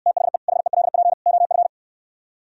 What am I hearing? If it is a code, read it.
Morse code, heard as a keyed tone, about 68 wpm, 700 Hz. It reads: THE HELP OF